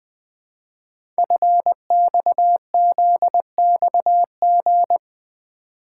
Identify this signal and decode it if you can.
Morse code, heard as a keyed tone, about 20 wpm, 700 Hz. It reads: FXZXG
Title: Code Groups - 5 Letters - 20wpm